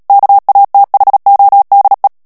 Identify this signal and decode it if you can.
Morse code, heard as a keyed tone, about 37 wpm, 800 Hz. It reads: KATHODE
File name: morse mp3 / HOLLANDAIS texte clair 2 vitesse 35 mots minute